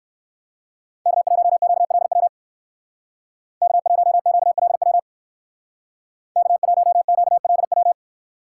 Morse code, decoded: K1QLW K1QLW K1QLW

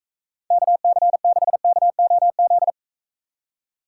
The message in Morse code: KC6KOZ

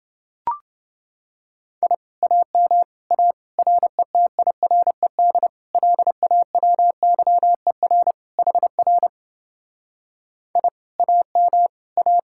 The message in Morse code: I AM A RETIRED LAWYER HR I AM A

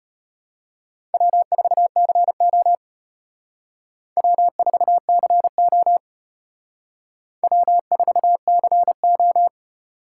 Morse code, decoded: W4CO W4CO W4CO

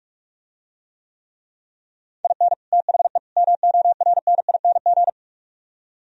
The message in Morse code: IN THE MORNING

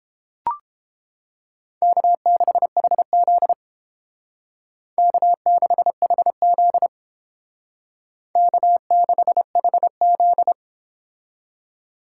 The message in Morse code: K6HZ K6HZ K6HZ

